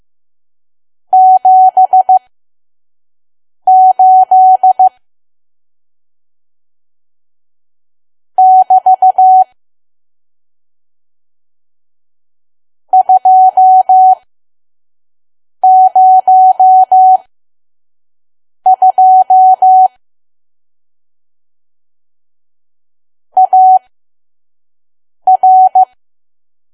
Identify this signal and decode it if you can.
Morse code, heard as a keyed tone, about 15 wpm, spaced as for 5 wpm, 750 Hz. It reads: 78 <BT> 202 AR
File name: Archive / 5 WPM files / PM